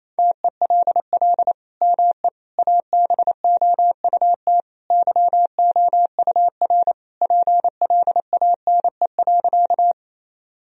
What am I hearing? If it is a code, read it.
Morse code, heard as a keyed tone, about 27 wpm, 700 Hz. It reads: TELL ME ABOUT YOUR PLANE.
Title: Sentences from Top 1000 Words - Review Part 3 - 28wpm